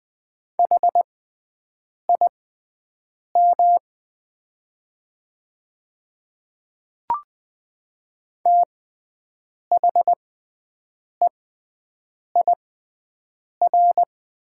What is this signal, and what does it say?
Morse code, heard as a keyed tone, about 20 wpm, spaced as for 7 wpm, 700 Hz. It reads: HIM THEIR
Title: 6x Character Spacing - Top 100 Words - 20wpm